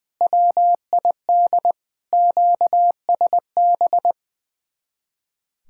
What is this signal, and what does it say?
Morse code, heard as a keyed tone, about 20 wpm, 700 Hz. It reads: WID QSB
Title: Speed-Racing Common QSO Phrases - 20wpm